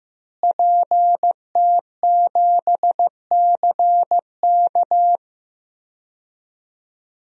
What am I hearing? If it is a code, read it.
Morse code, heard as a keyed tone, about 15 wpm, 700 Hz. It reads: PT7CK
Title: DXCC-100 Call Signs - 5 Characters - Repeated 2x - 15wpm